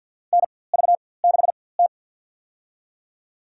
Morse code, heard as a keyed tone, about 47 wpm, spaced as for 23 wpm, 700 Hz. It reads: NV6T
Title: USA Call Signs - 4x Character Spacing - 50wpm